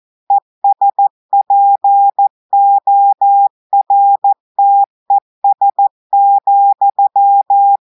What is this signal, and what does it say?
Morse code, heard as a keyed tone, about 14 wpm, 800 Hz. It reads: ESPORTES,
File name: audios / portugais / portugais texte en clair 2 vitesse 14 mots minutes